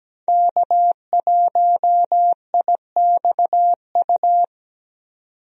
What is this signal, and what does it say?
Morse code, heard as a keyed tone, about 17 wpm, 700 Hz. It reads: K1IXU
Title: USA Call Signs - 5 Characters - 17wpm